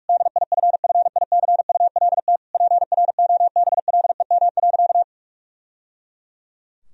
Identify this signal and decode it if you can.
Morse code, heard as a keyed tone, about 41 wpm, 700 Hz. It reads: DIFFICULT PROBLEM.
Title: Sentences from Top 1000 Words - Review Part 2 - 45wpm